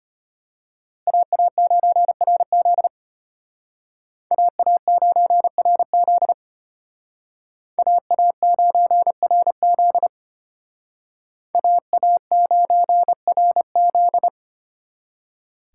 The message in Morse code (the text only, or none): AA9RZ AA9RZ AA9RZ AA9RZ